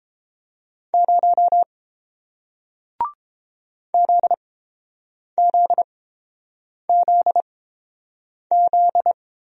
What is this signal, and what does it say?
Morse code, heard as a keyed tone, about 30 wpm, spaced as for 7 wpm, 700 Hz. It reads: 0 ZZZZ